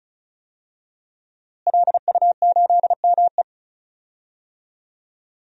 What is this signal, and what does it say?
Morse code, heard as a keyed tone, about 34 wpm, 700 Hz. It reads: LU8ME
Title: DXCC-100 Call Signs - 5 Characters - Repeated 2x - 35wpm